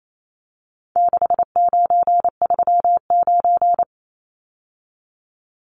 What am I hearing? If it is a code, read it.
Morse code, heard as a keyed tone, about 28 wpm, 700 Hz. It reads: 6939
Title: Numbers - 4-Digit 28wpm